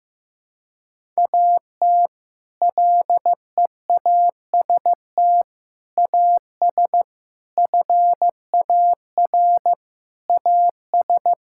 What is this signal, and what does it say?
Morse code, heard as a keyed tone, about 15 wpm, 700 Hz. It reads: AT LEAST AS FAR AS
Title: Sentences from Top 1000 Words - Review Part 6 - 15wpm